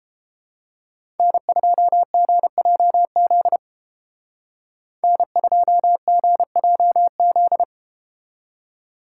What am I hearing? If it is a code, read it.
Morse code, heard as a keyed tone, about 31 wpm, 700 Hz. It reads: N2GJZ N2GJZ